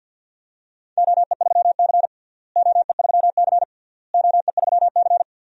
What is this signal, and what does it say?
Morse code, heard as a keyed tone, about 47 wpm, 700 Hz. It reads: OE3C OE3C OE3C
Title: DXCC-100 Call Signs - 4 Characters - Repeated 3x - 50wpm